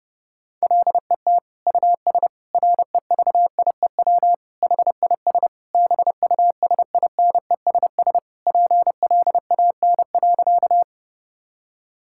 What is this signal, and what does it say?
Morse code, heard as a keyed tone, about 32 wpm, 700 Hz. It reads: LET US REVIEW HIS BUSINESS PLAN.
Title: Sentences from Top 1000 Words - Review Part 6 - 30wpm